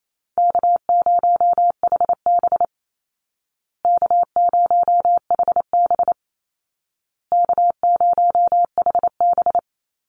Morse code, heard as a keyed tone, about 28 wpm, 700 Hz. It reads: K0HB K0HB K0HB
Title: Easy Call Signs Repeated 3x - 28wpm